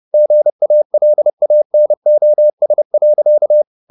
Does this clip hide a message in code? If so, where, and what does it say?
Yes: Morse code from 0.1 to 3.6 s, heard as a keyed tone, about 30 wpm, 600 Hz. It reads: GALANOS.